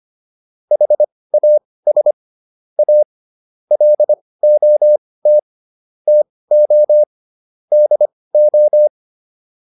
Morse code, heard as a keyed tone, about 25 wpm, spaced as for 18 wpm, 600 Hz. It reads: HAS A LOT TO DO